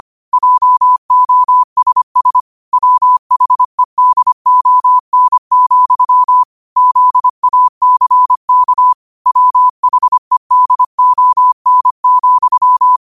Morse code, heard as a keyed tone, about 25 wpm, 1000 Hz. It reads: JOSS WHEDON, ZACK WHEDON,
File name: qotdfiles / 25wpm